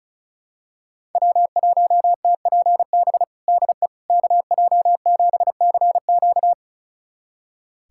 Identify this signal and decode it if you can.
Morse code, heard as a keyed tone, about 34 wpm, 700 Hz. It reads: W1TPB DE KJ7CQ
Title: USA Callsign DE Callsign - 5 Characters - 35wpm